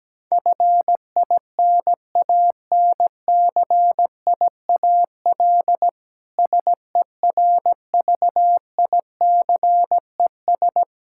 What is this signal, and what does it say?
Morse code, heard as a keyed tone, about 17 wpm, 700 Hz. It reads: FINANCIAL SERVICES
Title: Practice QSO - Exchange 3 - More Info - 17wpm